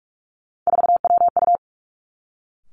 Morse code, heard as a keyed tone, about 46 wpm, 700 Hz. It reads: 4WU
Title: Three Challenging Letter-Number Combinations 45wpm